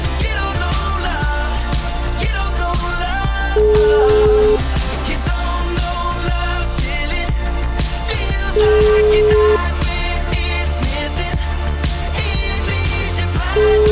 Ringback tone (Europe, 425 Hz), with noise.